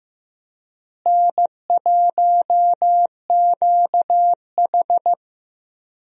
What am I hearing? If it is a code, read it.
Morse code, heard as a keyed tone, about 15 wpm, 700 Hz. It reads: N1QH
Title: USA Call Signs - 4 Characters - Repeated 3x - 15wpm